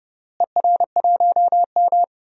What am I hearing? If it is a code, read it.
Morse code, heard as a keyed tone, about 31 wpm, 700 Hz. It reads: ER1M
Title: DXCC-100 Call Signs - 4 Characters - Repeated 3x - 30wpm